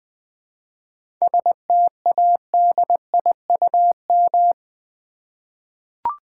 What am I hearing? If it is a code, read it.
Morse code, heard as a keyed tone, about 20 wpm, 700 Hz. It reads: STADIUM
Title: 7 Letter Words - First 3 Letters Given - 20wpm